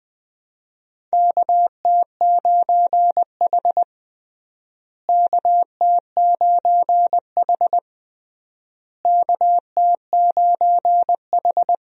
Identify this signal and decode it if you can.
Morse code, heard as a keyed tone, about 21 wpm, 700 Hz. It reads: KT9H KT9H KT9H
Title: USA Call Signs - 4 Characters - Repeated 3x - 20wpm